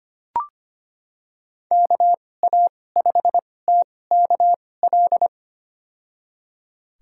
Morse code, heard as a keyed tone, about 25 wpm, spaced as for 18 wpm, 700 Hz. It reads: KA5TKL
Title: USA Call Signs - 6 Characters - 2x Character Spacing - 25wpm